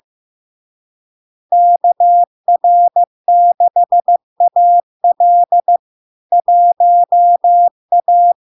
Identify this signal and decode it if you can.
Morse code, heard as a keyed tone, about 15 wpm, 700 Hz. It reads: KR6AL 1A
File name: LARK-FD-2022-Log-in-CW-15wpm-700-Hz